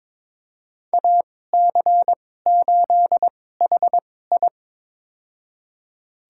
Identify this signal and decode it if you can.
Morse code, heard as a keyed tone, about 22 wpm, spaced as for 16 wpm, 700 Hz. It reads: AC8HI
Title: USA Call Signs - 2x Character Spacing - 22wpm